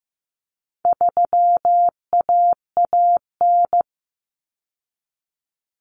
Morse code, heard as a keyed tone, about 15 wpm, 700 Hz. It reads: 3AAN
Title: Common Cut Numbers - 4-Digit 15wpm